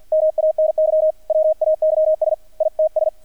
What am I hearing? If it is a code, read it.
Morse code, heard as a keyed tone, about 50 wpm, 630 Hz, with radio noise. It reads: MANY WAYS ITS
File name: archived / CRSnet